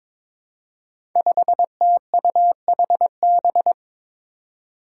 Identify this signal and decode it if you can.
Morse code, heard as a keyed tone, about 22 wpm, 700 Hz. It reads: 5TUHB
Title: Five Challenging Letter-Number Combinations 22wpm